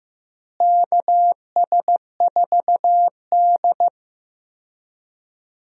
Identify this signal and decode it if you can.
Morse code, heard as a keyed tone, about 15 wpm, 700 Hz. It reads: KS4D
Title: USA Call Signs - 4 Characters - Repeated 3x - 15wpm